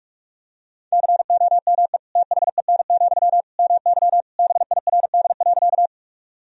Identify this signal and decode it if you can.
Morse code, heard as a keyed tone, about 46 wpm, 700 Hz. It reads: COME THEN, MY BIRD.
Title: Sentences from Top 1000 Words - Review Part 2 - 45wpm